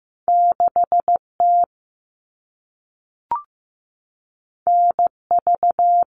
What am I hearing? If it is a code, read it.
Morse code, heard as a keyed tone, about 15 wpm, 700 Hz. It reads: 6T NV
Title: All Cut Numbers - 2-Digit 15wpm